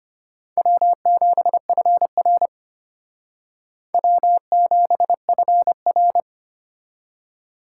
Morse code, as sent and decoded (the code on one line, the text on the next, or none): .-- --... ..-. .-. / .-- --... ..-. .-.
W7FR W7FR